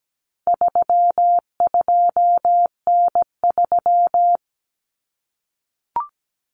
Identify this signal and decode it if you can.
Morse code, heard as a keyed tone, about 17 wpm, 700 Hz. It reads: 32N3
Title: Common Cut Numbers - 4-Digit 17wpm